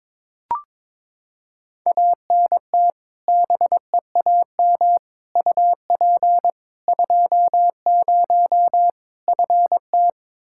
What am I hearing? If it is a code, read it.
Morse code, heard as a keyed tone, about 22 wpm, 700 Hz. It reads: ANT BEAM UP 20 FT
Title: K1USN SST QSO Practice 22wpm